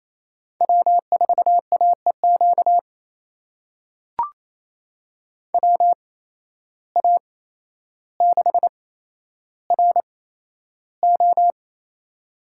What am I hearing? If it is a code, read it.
Morse code, heard as a keyed tone, about 28 wpm, spaced as for 12 wpm, 700 Hz. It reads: W4AEQ WA6RO
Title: USA Call Signs - 5 Characters - 8x Character Spacing - 28wpm